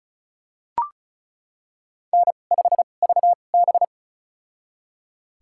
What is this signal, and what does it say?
Morse code, heard as a keyed tone, about 36 wpm, spaced as for 25 wpm, 700 Hz. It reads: N5VB
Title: USA Call Signs - 4 Characters - 2x Character Spacing - 35wpm